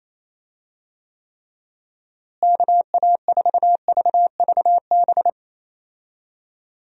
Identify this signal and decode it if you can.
Morse code, heard as a keyed tone, about 28 wpm, 700 Hz. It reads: KA4VVB
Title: USA Call Signs - 6 Characters - Repeated 2x - 28wpm